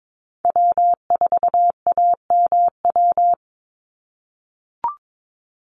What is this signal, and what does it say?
Morse code, heard as a keyed tone, about 22 wpm, 700 Hz. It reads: W4AMW
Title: NA SOTA Chaser Callsigns 22wpm